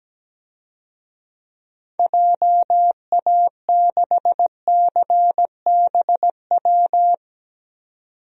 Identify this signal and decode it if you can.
Morse code, heard as a keyed tone, about 17 wpm, 700 Hz. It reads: JA6CBW
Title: DXCC-100 Call Signs - 6 Characters - 17wpm